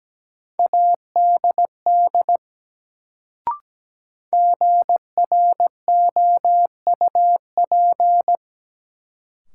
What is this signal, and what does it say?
Morse code, heard as a keyed tone, about 17 wpm, 700 Hz. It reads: ADD GROUP
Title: Sentences from Top 1000 Words - Review Part 1 - New Word Drill - 17wpm